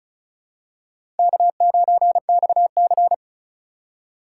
Morse code, heard as a keyed tone, about 36 wpm, 700 Hz. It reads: K9XC
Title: USA Call Signs - 4 Characters - 3x Character Spacing - 35wpm